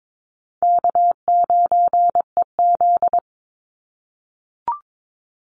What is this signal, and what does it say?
Morse code, heard as a keyed tone, about 22 wpm, 700 Hz. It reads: K9EZ
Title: NA SOTA Chaser Callsigns 22wpm